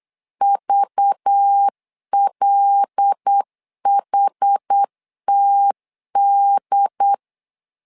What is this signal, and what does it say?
Morse code, heard as a keyed tone, about 8 wpm, 800 Hz. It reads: VLHTD
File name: LEZIONI / 5 LBVC / 5 LBV 8wpm 2837